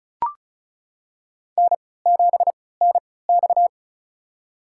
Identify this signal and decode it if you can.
Morse code, heard as a keyed tone, about 34 wpm, spaced as for 20 wpm, 700 Hz. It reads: N7NX
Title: USA Call Signs - 3x Character Spacing - 35wpm